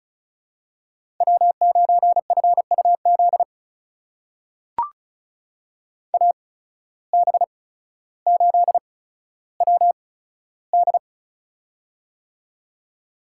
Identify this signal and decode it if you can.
Morse code, heard as a keyed tone, about 36 wpm, spaced as for 15 wpm, 700 Hz. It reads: W9FUZ AB8WD
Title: USA Call Signs - 5 Characters - 8x Character Spacing - 35wpm